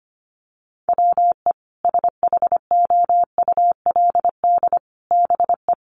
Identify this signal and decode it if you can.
Morse code, heard as a keyed tone, about 25 wpm, 700 Hz. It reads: WE SHOULD BE
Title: Sets of 5 Words Encore2 25wpm